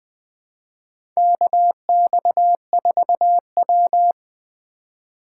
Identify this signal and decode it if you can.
Morse code, heard as a keyed tone, about 20 wpm, 700 Hz. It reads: KX4W
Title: USA Call Signs - 4 Characters - 4x Character Spacing - 20wpm